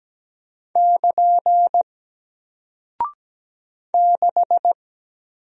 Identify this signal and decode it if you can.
Morse code, heard as a keyed tone, about 17 wpm, 700 Hz. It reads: <KN> 6
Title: Single Character (Letter:Number:Punctuation:Prosign) Alt 17wpm